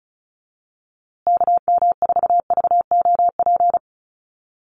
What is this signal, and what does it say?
Morse code, heard as a keyed tone, about 34 wpm, 700 Hz. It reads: KM4VOP
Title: USA Call Signs Repeated 2x 35wpm